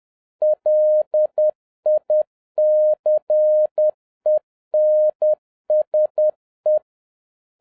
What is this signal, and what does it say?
Morse code, heard as a keyed tone, about 10 wpm, 610 Hz. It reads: LICENSE